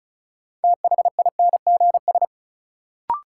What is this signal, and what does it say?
Morse code, heard as a keyed tone, about 36 wpm, 700 Hz. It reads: THINGS
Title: Sentences from Top 1400 Words - New Word Drill Alt - 35wpm